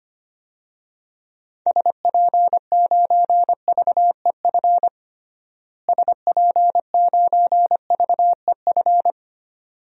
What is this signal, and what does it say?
Morse code, heard as a keyed tone, about 25 wpm, 700 Hz. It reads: SP9VEF SP9VEF
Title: DXCC-100 Call Signs - 6 Characters - Repeated 2x - 25wpm